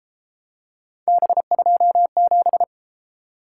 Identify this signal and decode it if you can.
Morse code, heard as a keyed tone, about 32 wpm, 700 Hz. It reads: B27